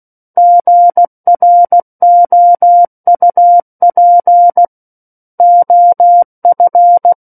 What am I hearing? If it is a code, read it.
Morse code, heard as a keyed tone, about 16 wpm, 700 Hz. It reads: GROUP OF
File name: Book One / 07-Chapter 7 - 16 WPM